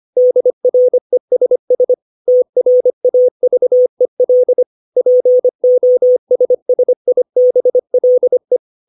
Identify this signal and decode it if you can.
Morse code, heard as a keyed tone, about 25 wpm, 500 Hz. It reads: DRESS TRAVEL POSSIBLE
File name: 632 words / 632 Words 25WPM